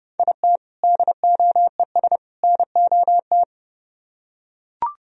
Morse code, heard as a keyed tone, about 29 wpm, 700 Hz. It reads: IT DOES NOT